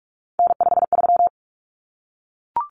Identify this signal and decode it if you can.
Morse code, heard as a keyed tone, about 44 wpm, 700 Hz. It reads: N53
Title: Common Cut Numbers - 3-Digit 45wpm